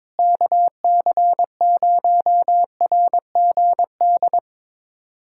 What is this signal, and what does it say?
Morse code, heard as a keyed tone, about 22 wpm, 700 Hz. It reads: KC0RGD
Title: USA Callsign DE Callsign - 6 Characters - 22wpm